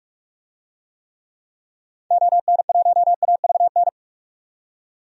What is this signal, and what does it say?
Morse code, heard as a keyed tone, about 44 wpm, 700 Hz. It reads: ON1AVN
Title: DXCC-100 Call Signs - 6 Characters - 45wpm